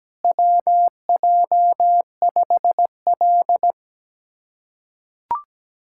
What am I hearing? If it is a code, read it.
Morse code, heard as a keyed tone, about 17 wpm, 700 Hz. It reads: WJ5L
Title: USA Callsign DE Callsign - 4 Characters - 17wpm